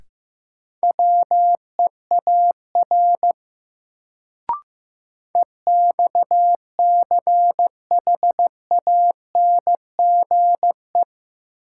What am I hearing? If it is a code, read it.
Morse code, heard as a keyed tone, about 15 wpm, 700 Hz. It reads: WEAR EXCHANGE
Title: Sentences from Top 1000 Words - Review Part 6 - New Word Drill - 15wpm